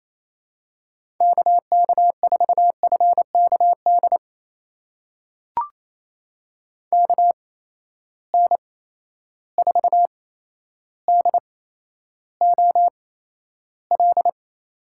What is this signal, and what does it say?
Morse code, heard as a keyed tone, about 28 wpm, spaced as for 12 wpm, 700 Hz. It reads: KK4FKD KN4DOL
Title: USA Call Signs - 6 Characters - 8x Character Spacing - 28wpm